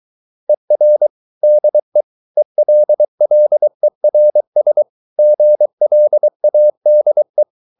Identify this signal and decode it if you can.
Morse code, heard as a keyed tone, about 23 wpm, 600 Hz. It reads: ER DE ELLERS GLADE
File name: Ferie QRS 23wpm